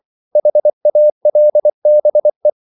Morse code, heard as a keyed tone, about 24 wpm, 600 Hz. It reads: HALBE